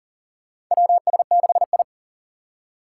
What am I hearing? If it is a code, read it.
Morse code, heard as a keyed tone, about 40 wpm, 700 Hz. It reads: WS6I
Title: USA Call Signs - 4 Characters - 6x Character Spacing - 40wpm